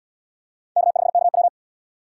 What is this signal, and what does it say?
Morse code, heard as a keyed tone, about 72 wpm, 700 Hz. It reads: K5KW